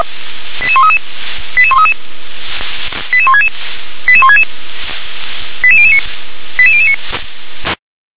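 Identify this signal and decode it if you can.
Five-tone selective call (ZVEI), heard as quick runs of five tones, tones of about 80 ms. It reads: 88133 88133 88166 88166 88009 88009
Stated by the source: Five-tone ZVEI selective call 88020, received on an SDR.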